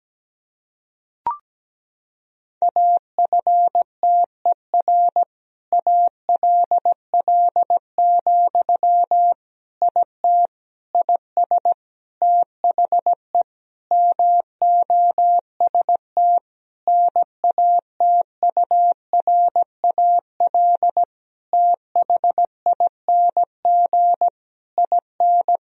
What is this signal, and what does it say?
Morse code, heard as a keyed tone, about 17 wpm, 700 Hz. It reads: AFTER ALL, IT IS THE MOST NATURAL THING IN
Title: Sentences from Top 1000 Words - Review Part 4 - 17wpm